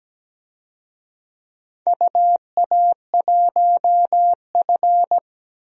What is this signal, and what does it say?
Morse code, heard as a keyed tone, about 17 wpm, 700 Hz. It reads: UA1F
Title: DXCC-100 Call Signs - 4 Characters - Repeated 2x - 17wpm